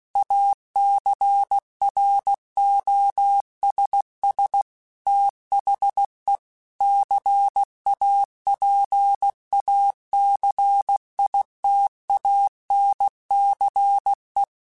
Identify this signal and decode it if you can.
Morse code, heard as a keyed tone, about 16 wpm, 800 Hz. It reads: ACROSS THE CAPACITANCE